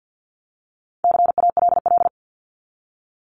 Morse code, heard as a keyed tone, about 48 wpm, 700 Hz. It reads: CALL
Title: Top 300 Words 50wpm